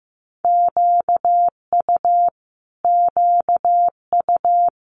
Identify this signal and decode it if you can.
Morse code, heard as a keyed tone, about 15 wpm, 700 Hz. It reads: QU QU
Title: Multi-letter Phonemes Repeated 3x 15wpm